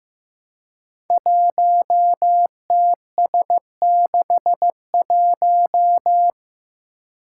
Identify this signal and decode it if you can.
Morse code, heard as a keyed tone, about 15 wpm, 700 Hz. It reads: 1TS61
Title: Five Challenging Letter-Number Combinations 15wpm